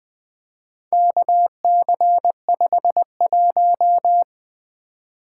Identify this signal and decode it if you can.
Morse code, heard as a keyed tone, about 20 wpm, 700 Hz. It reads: KC51
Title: Four Letter-Number Combinations - 20wpm